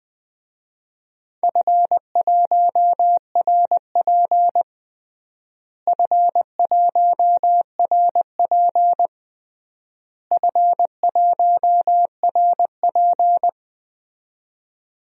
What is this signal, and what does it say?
Morse code, heard as a keyed tone, about 20 wpm, 700 Hz. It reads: F1RP F1RP F1RP
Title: DXCC-100 Call Signs - 4 Characters - Repeated 3x - 20wpm